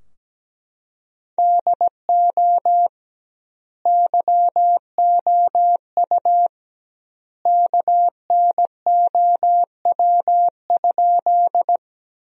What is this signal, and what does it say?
Morse code, heard as a keyed tone, about 17 wpm, 700 Hz. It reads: DO YOU KNOW?
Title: 2x Word-Spacing - Sentences from Top 100 Words - 17wpm